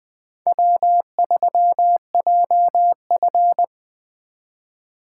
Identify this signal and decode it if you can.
Morse code, heard as a keyed tone, about 20 wpm, 700 Hz. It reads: W3JF
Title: Speed Racing - USA Call Signs 17wpm